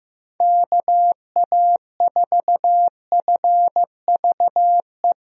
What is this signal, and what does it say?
Morse code, heard as a keyed tone, about 15 wpm, 700 Hz. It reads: KA4FVE
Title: USA Call Signs - 6 Characters - 15wpm